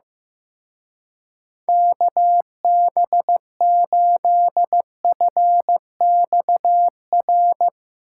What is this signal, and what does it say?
Morse code, heard as a keyed tone, about 16 wpm, 700 Hz. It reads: KB8FXR
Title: Speed Racing - USA 2x3 Call Signs 15wpm